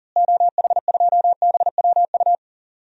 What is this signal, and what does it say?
Morse code, heard as a keyed tone, about 39 wpm, 700 Hz. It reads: OH2BWU
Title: DXCC-100 Call Signs - 6 Characters - Repeated 2x - 40wpm